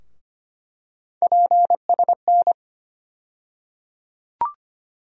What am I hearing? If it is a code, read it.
Morse code, heard as a keyed tone, about 25 wpm, 700 Hz. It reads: PSN
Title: USA Call Sign Postfixes - 3 Letters 25wpm